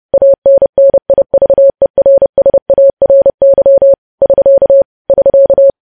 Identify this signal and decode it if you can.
Morse code, heard as a keyed tone, about 30 wpm, 550 Hz. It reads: ANNIVERSARY <SK> <SK>
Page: News Headlines in Morse Code at 30 WPM